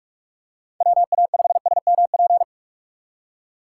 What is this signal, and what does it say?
Morse code, heard as a keyed tone, about 46 wpm, 700 Hz. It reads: WA5SMP
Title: USA Call Signs - 6 Characters - 3x Character Spacing - 45wpm